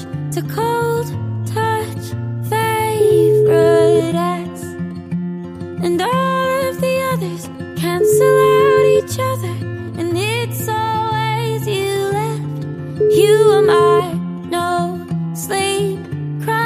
Ringback tone (Europe, 425 Hz).